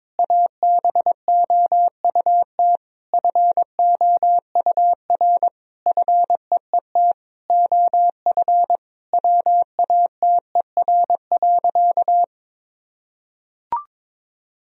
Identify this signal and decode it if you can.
Morse code, heard as a keyed tone, about 22 wpm, 700 Hz. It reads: ABOUT FOUR FEET OF WATER.
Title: Sentences from Top 1000 Words - Review Part 1 - 22wpm